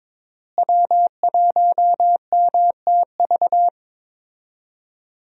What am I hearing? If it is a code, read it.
Morse code, heard as a keyed tone, about 22 wpm, 700 Hz. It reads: W1MTV
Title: USA Call Signs - 5 Characters - Repeated 3x - 22wpm